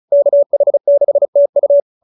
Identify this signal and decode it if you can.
Morse code, heard as a keyed tone, about 34 wpm, 570 Hz. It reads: KH6TU